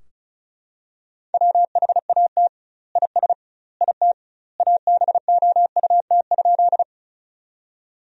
Morse code, heard as a keyed tone, about 34 wpm, 700 Hz. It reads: WHAT IS IT ABOUT?
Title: 2x Word-Spacing - Sentences from Top 100 Words - 35wpm